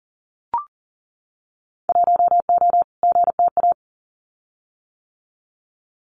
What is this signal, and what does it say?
Morse code, heard as a keyed tone, about 42 wpm, 700 Hz. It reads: 1O GTA
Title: Winter Field Day Exchanges 40wpm